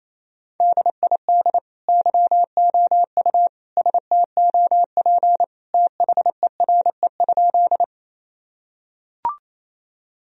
Morse code, heard as a keyed tone, about 28 wpm, 700 Hz. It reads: DID YOU STOP THERE?
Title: Sentences from Top 1000 Words - Review Part 1 - 28wpm